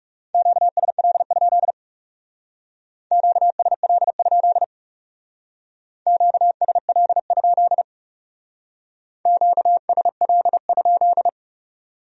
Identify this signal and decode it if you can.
Morse code, heard as a keyed tone, about 40 wpm, 700 Hz. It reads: QSL? QSL? QSL? QSL?